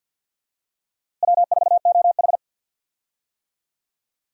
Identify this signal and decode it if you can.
Morse code, heard as a keyed tone, about 46 wpm, 700 Hz. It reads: W4OH
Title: USA Call Signs - 4 Characters - Repeated 3x - 50wpm